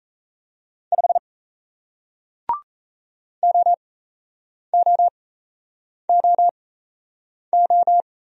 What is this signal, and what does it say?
Morse code, heard as a keyed tone, about 38 wpm, spaced as for 8 wpm, 700 Hz. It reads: 5 OOOO